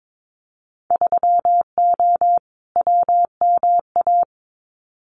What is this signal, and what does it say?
Morse code, heard as a keyed tone, about 22 wpm, 700 Hz. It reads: 3O WMA
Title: Winter Field Day Exchanges Encore 22wpm